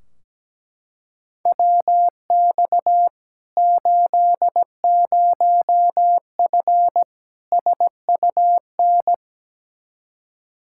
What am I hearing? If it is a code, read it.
Morse code, heard as a keyed tone, about 17 wpm, 700 Hz. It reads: WX 80F SUN
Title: Practice QSO - Exchange 2 - Info - 17wpm